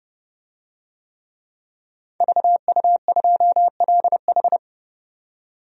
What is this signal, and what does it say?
Morse code, heard as a keyed tone, about 32 wpm, 700 Hz. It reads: VU2LH
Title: DXCC-100 Call Signs - 5 Characters - 30wpm